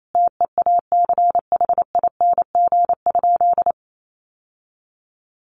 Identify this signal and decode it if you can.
Morse code, heard as a keyed tone, about 28 wpm, 700 Hz. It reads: TEACHING?
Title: Sentences from Top 1100 Words 28wpm